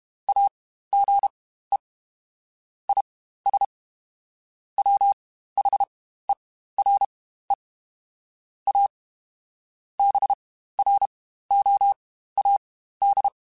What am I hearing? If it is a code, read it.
Morse code, heard as a keyed tone, about 32 wpm, spaced as for 15 wpm, 800 Hz. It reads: AGE IS WHERE A BROAD
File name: Spaced / Chuckles Spaced-mp3s / MiddleAge sp @30